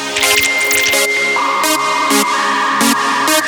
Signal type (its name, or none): dial-up modem handshake